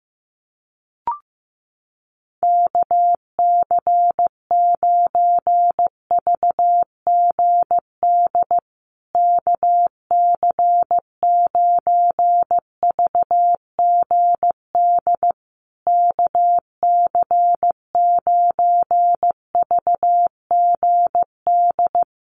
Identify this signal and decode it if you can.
Morse code, heard as a keyed tone, about 15 wpm, 700 Hz. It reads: KC9VGD KC9VGD KC9VGD
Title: USA Call Signs Repeated 3x 15wpm